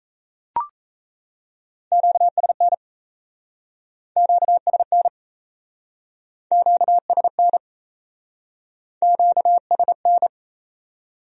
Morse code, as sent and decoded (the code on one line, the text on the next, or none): --.- ... -. / --.- ... -. / --.- ... -. / --.- ... -.
QSN QSN QSN QSN